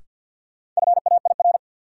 Morse code, heard as a keyed tone, about 52 wpm, 700 Hz. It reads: FAIR
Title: Sentences from Top 1000 Words - Review Part 5 - New Word Drill - 50wpm